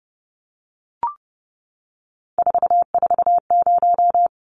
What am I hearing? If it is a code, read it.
Morse code, heard as a keyed tone, about 30 wpm, 700 Hz. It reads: 440
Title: Numbers - 3-Digit 30wpm